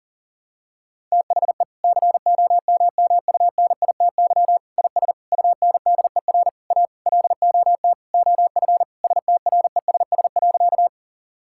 Morse code, heard as a keyed tone, about 37 wpm, 700 Hz. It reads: THE COMMUNITY IS UNDER A LOT OF STRESS.
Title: Sentences from Top 1000 Words - Review Part 6 - 40wpm